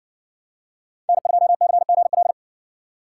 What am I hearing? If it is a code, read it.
Morse code, heard as a keyed tone, about 60 wpm, 700 Hz. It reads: N2CGL